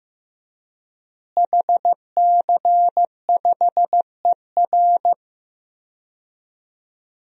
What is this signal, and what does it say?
Morse code, heard as a keyed tone, about 15 wpm, 700 Hz. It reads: HC5ER
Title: DXCC-100 Call Signs - 5 Characters - Repeated 2x - 15wpm